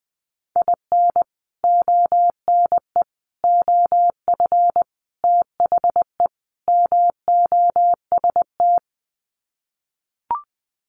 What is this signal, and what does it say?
Morse code, heard as a keyed tone, about 20 wpm, 700 Hz. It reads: IN ONE OF THE MOST
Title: Sets of 5 Words Encore 20wpm